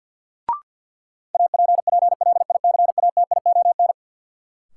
Morse code, heard as a keyed tone, about 53 wpm, 700 Hz. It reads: APPLICATION
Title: Sentences from Top 1200 Words - New Word Drill - 50wpm